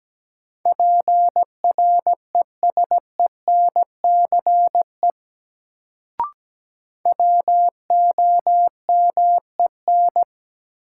Morse code, heard as a keyed tone, about 17 wpm, 700 Hz. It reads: PRESENCE WOMEN
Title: Sets of 3 Words - New Word Drill - 17wpm